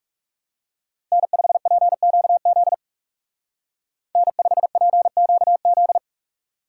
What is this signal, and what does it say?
Morse code, heard as a keyed tone, about 40 wpm, 700 Hz. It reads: N5PQZ N5PQZ